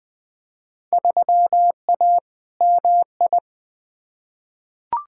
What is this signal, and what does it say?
Morse code, heard as a keyed tone, about 21 wpm, 700 Hz. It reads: 3A MI
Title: Field Day Exchanges - 20wpm